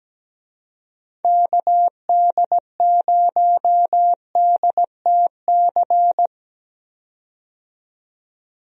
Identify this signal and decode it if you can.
Morse code, heard as a keyed tone, about 17 wpm, 700 Hz. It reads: KD0DTC
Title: USA Call Signs - 6 Characters - 17wpm